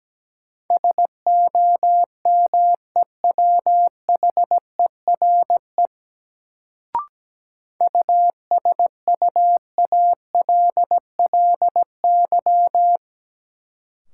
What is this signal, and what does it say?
Morse code, heard as a keyed tone, about 17 wpm, 700 Hz. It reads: SOMEWHERE USUALLY
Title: Sentences from Top 1400 Words - New Word Drill Alt - 17wpm